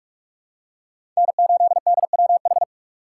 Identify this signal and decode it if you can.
Morse code, heard as a keyed tone, about 46 wpm, 700 Hz. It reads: N8DWH
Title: USA Call Signs - 5 Characters - 2x Character Spacing - 45wpm